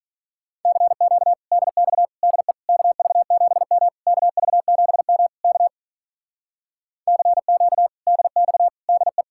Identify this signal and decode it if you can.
Morse code, heard as a keyed tone, about 43 wpm, 700 Hz. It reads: CQ DX DE KV7M KV7M K CQ DX DE
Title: Speed-Racing Common QSO Phrases Encore - 35wpm